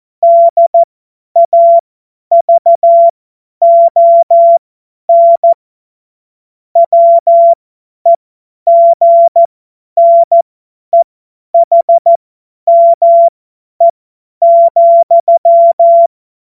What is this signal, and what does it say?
Morse code, heard as a keyed tone, about 14 wpm, spaced as for 10 wpm, 680 Hz. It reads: DAVON WEGNEHME,
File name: Morsekurs / Witze / Witz 10. August 2021, 50 F70